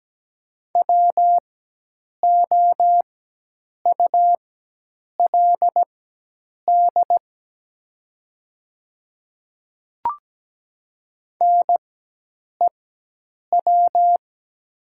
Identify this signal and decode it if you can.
Morse code, heard as a keyed tone, about 17 wpm, spaced as for 8 wpm, 700 Hz. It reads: WOULD NEW
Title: 4x Character Spacing - Top 200 Words - 17wpm